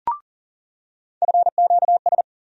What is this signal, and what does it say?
Morse code, heard as a keyed tone, about 40 wpm, 700 Hz. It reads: FQS
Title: USA Call Sign Postfixes - 3 Letters 40wpm